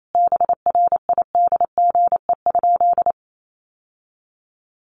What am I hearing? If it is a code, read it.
Morse code, heard as a keyed tone, about 28 wpm, 700 Hz. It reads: BRIDGE?
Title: Sentences from Top 1200 Words 28wpm